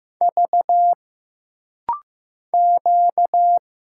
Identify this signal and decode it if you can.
Morse code, heard as a keyed tone, about 15 wpm, 700 Hz. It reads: V Q
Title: Single Letters - Rapid-Fire 15wpm